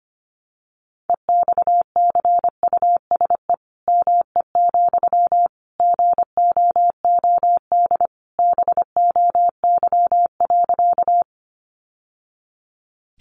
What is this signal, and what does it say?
Morse code, heard as a keyed tone, about 25 wpm, 700 Hz. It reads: EXCUSE ME, GOOD BOY.
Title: Sentences from Top 700 Words 25wpm